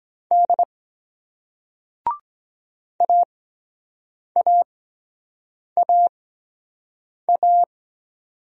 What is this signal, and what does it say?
Morse code, heard as a keyed tone, about 23 wpm, spaced as for 7 wpm, 700 Hz. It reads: D AAAA